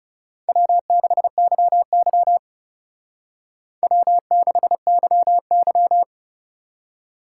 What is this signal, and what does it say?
Morse code, heard as a keyed tone, about 36 wpm, 700 Hz. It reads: W6YY W6YY
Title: Speed Racing - USA Call Signs 30wpm